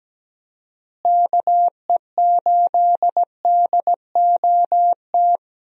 Morse code, heard as a keyed tone, about 17 wpm, 700 Hz. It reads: KE8DOT
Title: USA Call Signs - 6 Characters - Repeated 3x - 17wpm